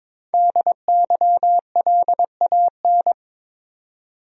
Morse code, heard as a keyed tone, about 22 wpm, 700 Hz. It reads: DYLAN
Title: USA Common Names - 22wpm